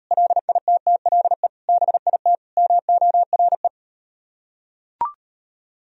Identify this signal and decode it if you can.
Morse code, heard as a keyed tone, about 38 wpm, 700 Hz. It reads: LITTLE BIT MORE